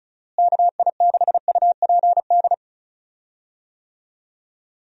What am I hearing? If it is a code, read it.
Morse code, heard as a keyed tone, about 34 wpm, 700 Hz. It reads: KI6UPD
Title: USA Call Signs - 6 Characters - Repeated 3x - 35wpm